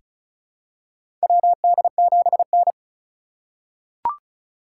Morse code, heard as a keyed tone, about 36 wpm, 700 Hz. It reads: WD7N
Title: USA Call Signs - 4 Characters - 2x Character Spacing - 35wpm